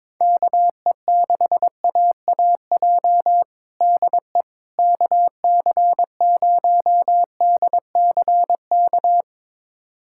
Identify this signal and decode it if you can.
Morse code, heard as a keyed tone, about 21 wpm, 700 Hz. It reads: KE6AAJ DE KC0DCK
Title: USA Callsign DE Callsign - 6 Characters - 22wpm